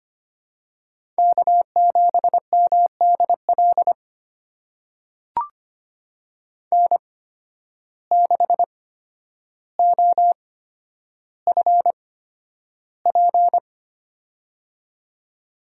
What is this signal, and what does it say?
Morse code, heard as a keyed tone, about 25 wpm, spaced as for 11 wpm, 700 Hz. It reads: K7MDL N6OFP
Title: USA Call Signs - 8x Character Spacing - 25wpm